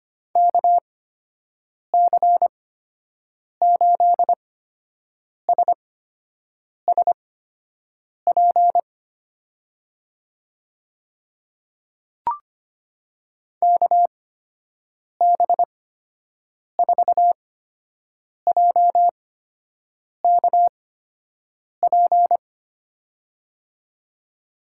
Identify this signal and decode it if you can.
Morse code, heard as a keyed tone, about 25 wpm, spaced as for 7 wpm, 700 Hz. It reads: KC8SSP KB4JKP
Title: USA Call Signs - 6 Characters - 8x Character Spacing - 25wpm